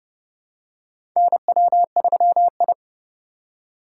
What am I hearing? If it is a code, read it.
Morse code, heard as a keyed tone, about 29 wpm, 690 Hz. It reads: NW3I
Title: USA Call Signs - 4x Character Spacing - 30wpm